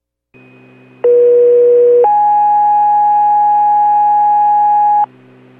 Two-tone page, heard as a one-second tone, then a three-second tone, about 480 then 830 Hz, with noise. It is heard from 1.0 to 5.0 s.